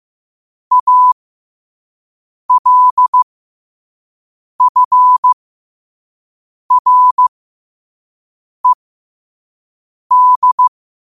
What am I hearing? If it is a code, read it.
Morse code, heard as a keyed tone, about 15 wpm, spaced as for 5 wpm, 1000 Hz. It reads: ALFRED
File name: qotdfiles / 5wpm